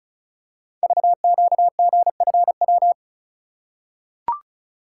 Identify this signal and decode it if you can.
Morse code, heard as a keyed tone, about 36 wpm, 700 Hz. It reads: VQGFW
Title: Code Groups - 5 Letters - 35wpm